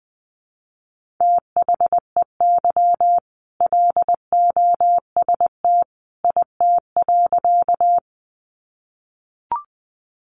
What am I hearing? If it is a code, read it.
Morse code, heard as a keyed tone, about 20 wpm, 700 Hz. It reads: THEY LOST IT.
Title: Sentences from Top 1400 Words 20wpm